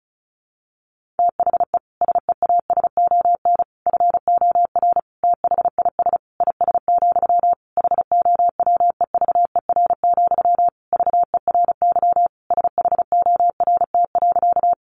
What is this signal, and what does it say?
Morse code, heard as a keyed tone, about 34 wpm, 700 Hz. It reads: THE SEASON FOR THIS IS, HOWEVER, VERY SHORT.
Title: Sentences from Top 1400 Words 35wpm